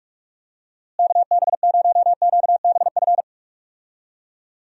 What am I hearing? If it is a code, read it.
Morse code, heard as a keyed tone, about 42 wpm, 700 Hz. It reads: KB0QBF
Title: USA Call Signs - 6 Characters - Repeated 2x - 45wpm